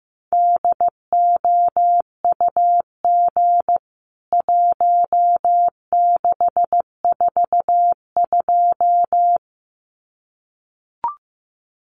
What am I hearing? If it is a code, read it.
Morse code, heard as a keyed tone, about 15 wpm, 700 Hz. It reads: DOUG 1642
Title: CWT Exchange 15wpm